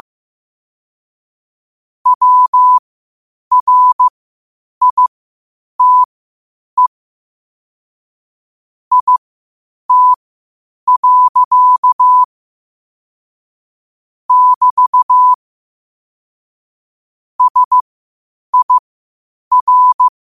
Morse code, heard as a keyed tone, about 15 wpm, spaced as for 8 wpm, 1000 Hz. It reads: WRITE IT. <BT> SIR
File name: qotdfiles / 8wpm